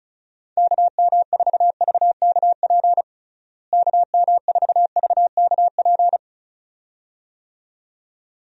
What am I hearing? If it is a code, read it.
Morse code, heard as a keyed tone, about 34 wpm, 700 Hz. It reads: KM4VKP KM4VKP
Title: USA Call Signs - 6 Characters - Repeated 2x - 35wpm